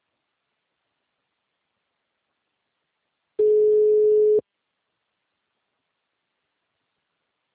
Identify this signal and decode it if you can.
Ringback tone (Europe, 425 Hz).
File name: out- -2007- - - .98542